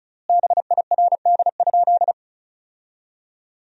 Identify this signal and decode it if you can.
Morse code, heard as a keyed tone, about 34 wpm, 700 Hz. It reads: BIRD?
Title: 3x Word-Spacing - Sentences from Top 400 Words - 35wpm